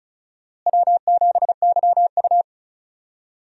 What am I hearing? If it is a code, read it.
Morse code, heard as a keyed tone, about 34 wpm, 700 Hz. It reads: W7YU
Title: USA Call Signs - 4 Characters - Repeated 3x - 35wpm